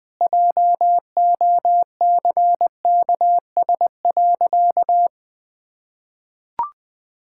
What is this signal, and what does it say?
Morse code, heard as a keyed tone, about 19 wpm, 700 Hz. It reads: JOCKS.
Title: Pangram Challenge v2 - 20wpm